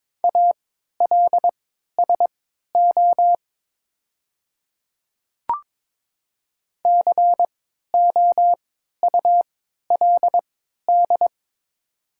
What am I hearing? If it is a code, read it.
Morse code, heard as a keyed tone, about 22 wpm, spaced as for 13 wpm, 690 Hz. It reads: ALSO COULD
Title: 3x Character Spacing - Top 200 Words - 22wpm